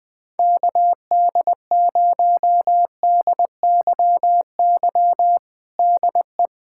Morse code, heard as a keyed tone, about 21 wpm, 700 Hz. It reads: KD0DYY DE
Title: USA Callsign DE Callsign - 6 Characters - 20wpm